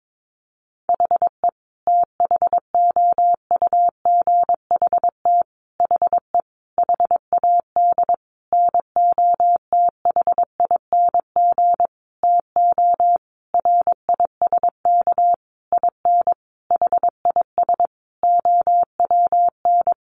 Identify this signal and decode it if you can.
Morse code, heard as a keyed tone, about 22 wpm, 700 Hz. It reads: HE THOUGHT HE HAD NOTHING TO RISK IN HIS OWN
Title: Sentences from Top 600 Words 22wpm